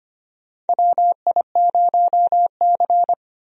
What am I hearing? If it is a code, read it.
Morse code, heard as a keyed tone, about 25 wpm, 700 Hz. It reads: WI0C
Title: USA Call Signs - 4 Characters - 25wpm